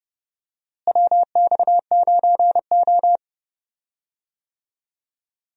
Morse code, heard as a keyed tone, about 30 wpm, 700 Hz. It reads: WX9O
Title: USA Call Signs - 4 Characters - Repeated 2x - 30wpm